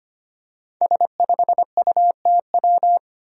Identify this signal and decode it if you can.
Morse code, heard as a keyed tone, about 25 wpm, 700 Hz. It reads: S5UTW
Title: Five Challenging Letter-Number Combinations 25wpm